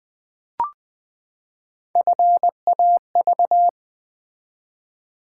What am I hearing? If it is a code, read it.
Morse code, heard as a keyed tone, about 20 wpm, 690 Hz. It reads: FAV
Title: Build A Word - 3 Letters At A Time - 9-Letter Words - 20wpm